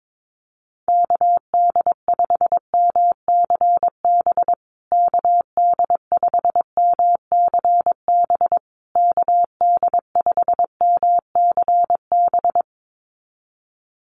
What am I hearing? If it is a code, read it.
Morse code, heard as a keyed tone, about 22 wpm, 700 Hz. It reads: KD5MCB KD5MCB KD5MCB
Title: USA Call Signs Repeated 3x 22wpm